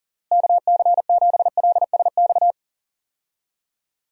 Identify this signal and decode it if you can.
Morse code, heard as a keyed tone, about 40 wpm, 700 Hz. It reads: KC7LSX
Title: USA Call Signs - 6 Characters - Repeated 3x - 40wpm